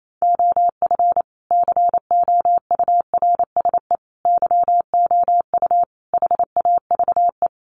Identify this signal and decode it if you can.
Morse code, heard as a keyed tone, about 28 wpm, 700 Hz. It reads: OF COURSE YOU HAVE
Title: Sentences from Top 1200 Words 28wpm